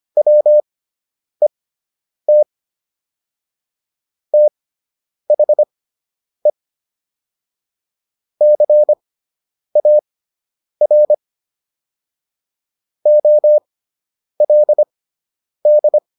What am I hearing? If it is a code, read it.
Morse code, heard as a keyed tone, about 25 wpm, spaced as for 9 wpm, 600 Hz. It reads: WET THE CAR OLD